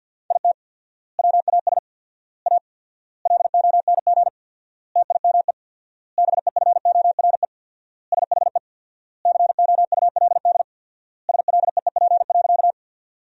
Morse code, heard as a keyed tone, about 47 wpm, 700 Hz. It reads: IT WAS A LONG TIME BEFORE SHE COULD SLEEP.
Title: 4x Word-Spacing - Sentences from Top 500 Words - 50wpm